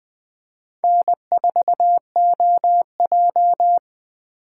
Morse code, heard as a keyed tone, about 20 wpm, 700 Hz. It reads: N4OJ
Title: USA Call Signs - 3x Character Spacing - 20wpm